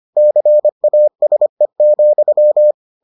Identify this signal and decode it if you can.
Morse code, heard as a keyed tone, about 25 wpm, 600 Hz. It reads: CASE,